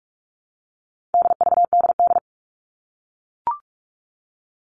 Morse code, heard as a keyed tone, about 44 wpm, 700 Hz. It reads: DVDD
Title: All Cut Numbers - 4-Digit 45wpm